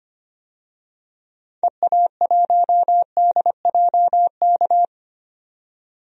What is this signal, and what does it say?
Morse code, heard as a keyed tone, about 25 wpm, 700 Hz. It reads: EA1DJK
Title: DXCC-100 Call Signs - 6 Characters - 25wpm